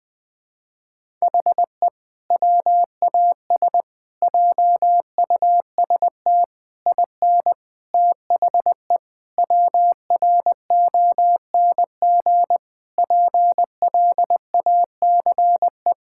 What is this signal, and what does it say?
Morse code, heard as a keyed tone, about 20 wpm, 700 Hz. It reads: HE WAS JUST IN THE WRONG PLACE
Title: Sentences with 7-Word Phrases - 20wpm